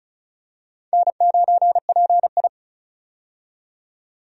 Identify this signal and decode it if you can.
Morse code, heard as a keyed tone, about 36 wpm, 700 Hz. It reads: N9PI
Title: USA Call Signs - 4 Characters - 35wpm